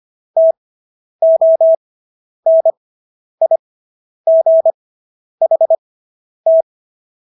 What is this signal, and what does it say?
Morse code, heard as a keyed tone, about 25 wpm, spaced as for 10 wpm, 650 Hz. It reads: TONIGHT